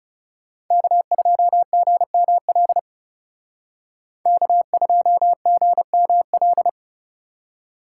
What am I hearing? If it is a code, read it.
Morse code, heard as a keyed tone, about 38 wpm, 700 Hz. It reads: K2GML K2GML